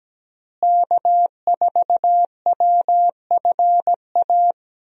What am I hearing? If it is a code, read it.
Morse code, heard as a keyed tone, about 17 wpm, 700 Hz. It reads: K4WFA
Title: USA Call Signs - 5 Characters - Repeated 2x - 17wpm